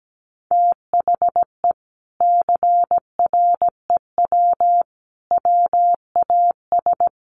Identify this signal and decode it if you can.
Morse code, heard as a keyed tone, about 17 wpm, 700 Hz. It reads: THE CREW WAS
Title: Sentences from Top 1500 Words 17wpm